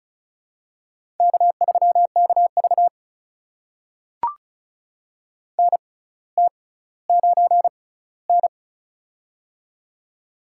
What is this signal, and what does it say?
Morse code, heard as a keyed tone, about 34 wpm, spaced as for 18 wpm, 700 Hz. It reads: K3KV NT9N
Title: USA Call Signs - 4 Characters - 6x Character Spacing - 35wpm